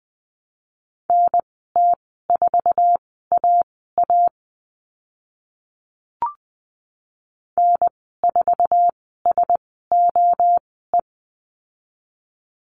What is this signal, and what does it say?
Morse code, heard as a keyed tone, about 20 wpm, spaced as for 15 wpm, 700 Hz. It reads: NT4AA N4SOE
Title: 062 Lesson 11g - Letter h - Call Signs 20wpm